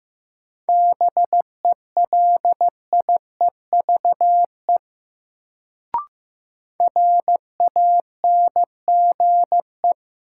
Morse code, heard as a keyed tone, about 15 wpm, 700 Hz. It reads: BELIEVE RANGE
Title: Sets of 4 Words - New Word Drill - 15wpm